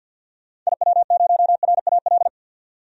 Morse code, heard as a keyed tone, about 51 wpm, 700 Hz. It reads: IW0RAL
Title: DXCC-100 Call Signs - 6 Characters - 50wpm